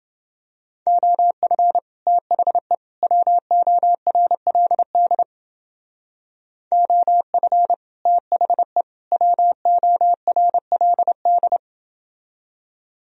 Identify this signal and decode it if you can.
Morse code, heard as a keyed tone, about 30 wpm, 700 Hz. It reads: OF THE WORLD OF THE WORLD